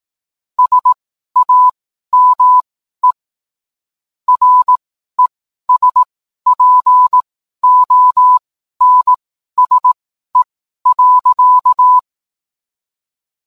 Morse code, heard as a keyed tone, about 18 wpm, spaced as for 12 wpm, 1000 Hz. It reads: SAME RESPONSE.